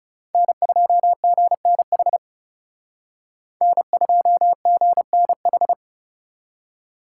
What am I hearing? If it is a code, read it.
Morse code, heard as a keyed tone, about 35 wpm, 700 Hz. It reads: N2GNH N2GNH
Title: Speed Racing - USA 1x3 Call Signs 30wpm